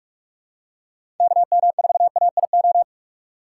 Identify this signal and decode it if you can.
Morse code, heard as a keyed tone, about 46 wpm, 700 Hz. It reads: KM4AIO
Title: USA Call Signs - 6 Characters - 2x Character Spacing - 45wpm